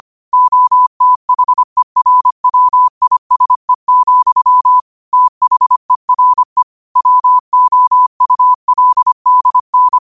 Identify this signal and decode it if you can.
Morse code, heard as a keyed tone, about 25 wpm, 1000 Hz. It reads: OTHERWISE, THERE WOULDN